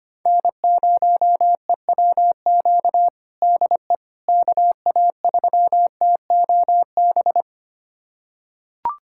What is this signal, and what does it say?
Morse code, heard as a keyed tone, about 25 wpm, 700 Hz. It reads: N0EWQ DE KA3TOB
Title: USA Callsign DE Callsign 25wpm